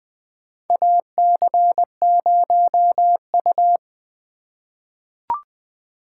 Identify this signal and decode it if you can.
Morse code, heard as a keyed tone, about 21 wpm, 700 Hz. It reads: AC0U
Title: USA Call Signs - 4 Characters - 2x Character Spacing - 20wpm